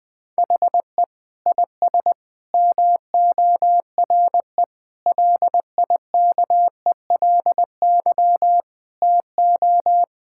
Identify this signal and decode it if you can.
Morse code, heard as a keyed tone, about 20 wpm, 700 Hz. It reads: HE IS MORE LIKELY TO
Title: Sentences with 3-Word Phrases - 20wpm